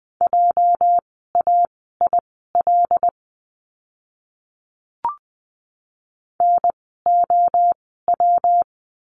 Morse code, heard as a keyed tone, about 20 wpm, spaced as for 15 wpm, 700 Hz. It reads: JAIL NOW
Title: 206 Lesson 33f - Letter j - Words 20wpm